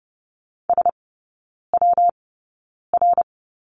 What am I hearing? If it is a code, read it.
Morse code, heard as a keyed tone, about 30 wpm, spaced as for 9 wpm, 700 Hz. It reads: SWR
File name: Lesson 19e - Letter w - Three Characters 30wpm